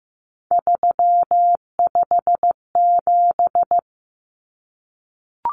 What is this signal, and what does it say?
Morse code, heard as a keyed tone, about 15 wpm, 700 Hz. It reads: 357
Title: Common Cut Numbers - 3-Digit 15wpm